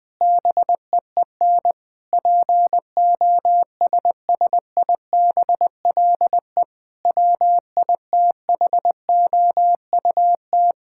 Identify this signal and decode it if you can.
Morse code, heard as a keyed tone, about 20 wpm, 700 Hz. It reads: BEEN POSSIBLE WITHOUT
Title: Sentences with 7-Word Phrases - Spoken Beginning - 20wpm